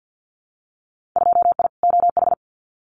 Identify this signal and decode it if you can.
Morse code, heard as a keyed tone, about 50 wpm, 700 Hz. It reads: 2I OH
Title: Winter Field Day Exchanges 50wpm